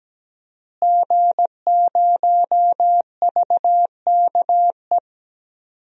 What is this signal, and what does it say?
Morse code, heard as a keyed tone, about 17 wpm, 700 Hz. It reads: G0VKE
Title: DXCC-100 Call Signs - 5 Characters - Repeated 3x - 17wpm